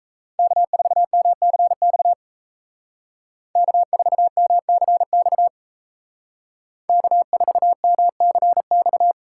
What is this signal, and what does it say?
Morse code, heard as a keyed tone, about 35 wpm, 700 Hz. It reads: K4MCX K4MCX K4MCX
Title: Speed Racing - USA 1x3 Call Signs 28wpm